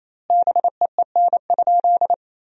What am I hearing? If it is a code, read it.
Morse code, heard as a keyed tone, about 28 wpm, 700 Hz. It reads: BEEN?
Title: 2x Word-Spacing - Sentences from Top 400 Words - 28wpm